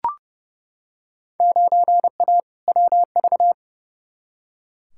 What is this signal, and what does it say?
Morse code, heard as a keyed tone, about 30 wpm, 700 Hz. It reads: 9A WV
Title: Field Day Exchanges - 30wpm